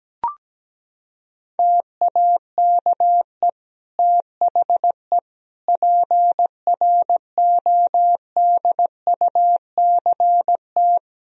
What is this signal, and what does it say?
Morse code, heard as a keyed tone, about 17 wpm, 700 Hz. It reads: TAKE THE PRODUCT
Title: Sentences from Top 1000 Words - Review Part 2 - 17wpm